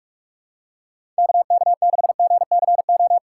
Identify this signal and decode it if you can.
Morse code, heard as a keyed tone, about 41 wpm, 700 Hz. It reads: KK6GCO
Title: USA Call Signs - 6x Character Spacing - 45wpm